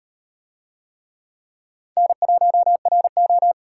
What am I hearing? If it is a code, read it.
Morse code, heard as a keyed tone, about 36 wpm, 700 Hz. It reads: N1RO